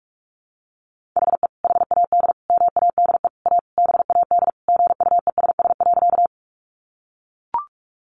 Morse code, heard as a keyed tone, about 44 wpm, 700 Hz. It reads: HE HAD MADE A BAD GUESS.
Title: Sentences from Top 1500 Words 45wpm